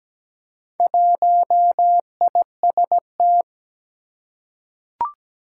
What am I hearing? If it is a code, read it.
Morse code, heard as a keyed tone, about 17 wpm, 700 Hz. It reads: 1IST
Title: Four Challenging Letter-Number Combinations 17wpm